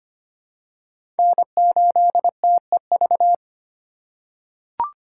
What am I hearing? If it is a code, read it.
Morse code, heard as a keyed tone, about 25 wpm, 700 Hz. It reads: N8TEV
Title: USA Call Signs - 5 Characters - 3x Character Spacing - 25wpm